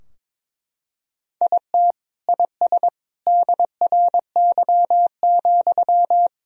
Morse code, heard as a keyed tone, about 22 wpm, 700 Hz. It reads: IT IS DRY,
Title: Sentences from Top 1000 Words - Review Part 3 - 22wpm